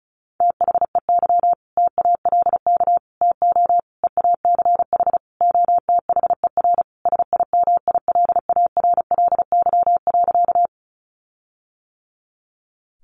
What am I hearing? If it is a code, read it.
Morse code, heard as a keyed tone, about 34 wpm, 700 Hz. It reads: THEY TALK TO EACH OTHER SIMILARLY.
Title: Sentences from Top 1400 Words 35wpm